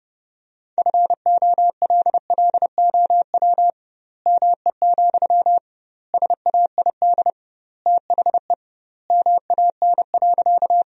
Morse code, heard as a keyed tone, about 31 wpm, 700 Hz. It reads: FOLLOW ME, SAID THE MAN.
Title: 2x Word-Spacing - Sentences from Top 200 Words - 30wpm